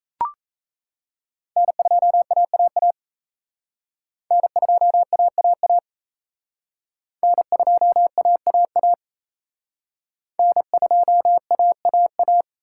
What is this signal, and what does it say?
Morse code, heard as a keyed tone, about 36 wpm, 700 Hz. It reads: N2AAA N2AAA N2AAA N2AAA